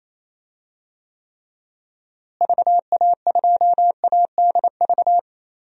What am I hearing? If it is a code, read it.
Morse code, heard as a keyed tone, about 28 wpm, 700 Hz. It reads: VA2ADV
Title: DXCC-100 Call Signs - 6 Characters - Repeated 3x - 28wpm